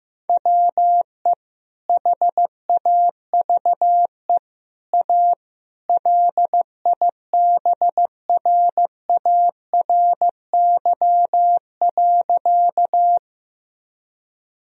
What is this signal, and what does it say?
Morse code, heard as a keyed tone, about 15 wpm, 700 Hz. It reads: WE HAVE A LIBRARY.
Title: Sentences from Top 1000 Words - Review Part 4 - 15wpm